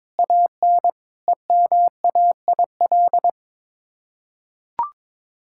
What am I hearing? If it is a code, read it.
Morse code, heard as a keyed tone, about 22 wpm, 700 Hz. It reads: AN EMAIL
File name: practice / 22wpm / Sets of 7 Words - 22wpm